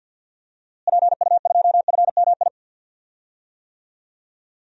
Morse code, heard as a keyed tone, about 46 wpm, 700 Hz. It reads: PU2FMI
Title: DXCC-100 Call Signs - 6 Characters - Repeated 3x - 50wpm